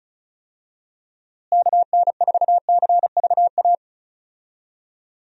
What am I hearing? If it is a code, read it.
Morse code, heard as a keyed tone, about 34 wpm, 700 Hz. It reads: KN4CVA
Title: USA Call Signs - 6 Characters - Repeated 2x - 35wpm